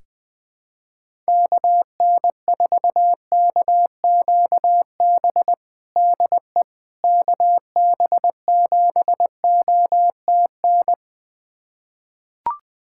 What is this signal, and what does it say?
Morse code, heard as a keyed tone, about 21 wpm, 700 Hz. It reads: KN4KQB DE KB7OTN
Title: USA Callsign DE Callsign 20wpm